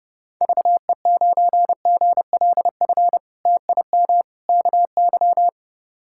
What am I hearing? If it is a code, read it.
Morse code, heard as a keyed tone, about 30 wpm, 700 Hz. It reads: VE9GLF TIM KY
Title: K1USN SST QSO Practice - 30wpm